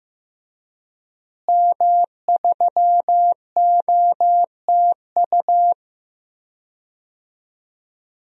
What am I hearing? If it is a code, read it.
Morse code, heard as a keyed tone, about 15 wpm, 700 Hz. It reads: M3OTU
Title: DXCC-100 Call Signs - 5 Characters - Repeated 3x - 15wpm